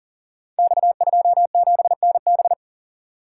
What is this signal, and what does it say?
Morse code, heard as a keyed tone, about 38 wpm, 700 Hz. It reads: X27NB
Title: Code Groups - 5 Letter:Numbers - 40wpm